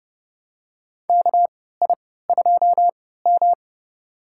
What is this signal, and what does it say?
Morse code, heard as a keyed tone, about 31 wpm, spaced as for 17 wpm, 700 Hz. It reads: KI2M
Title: USA Call Signs - 4 Characters - 3x Character Spacing - 30wpm